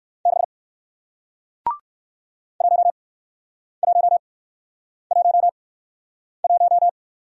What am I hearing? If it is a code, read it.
Morse code, heard as a keyed tone, about 60 wpm, spaced as for 9 wpm, 700 Hz. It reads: 6 1111